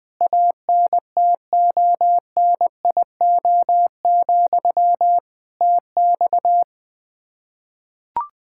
Morse code, heard as a keyed tone, about 20 wpm, 700 Hz. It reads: ANTONIO, TX
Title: K1USN SST QSO Practice 20wpm